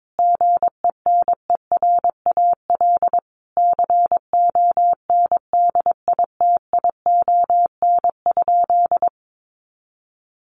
Morse code, heard as a keyed tone, about 22 wpm, 700 Hz. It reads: GENERAL CONDITION?
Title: Sentences from Top 1000 Words 22wpm